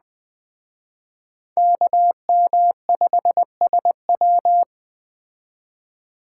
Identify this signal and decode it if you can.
Morse code, heard as a keyed tone, about 20 wpm, 700 Hz. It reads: KM5SW
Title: Speed Racing - USA 2x2 Call Signs 17wpm